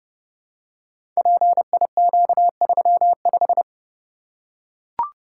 Morse code, heard as a keyed tone, about 30 wpm, 700 Hz. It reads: PIQ35
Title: Code Groups - 5 Letter:Numbers - 30wpm